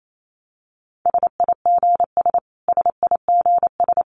Morse code, heard as a keyed tone, about 28 wpm, 700 Hz. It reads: SIGS SIGS
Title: Abbreviations Repeated 3x - 28wpm